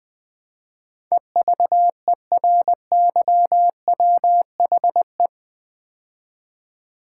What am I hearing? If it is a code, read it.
Morse code, heard as a keyed tone, about 20 wpm, 700 Hz. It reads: EVERYWHE
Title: Build A Word - 4 Letters At A Time - 10-Letter Words - 20wpm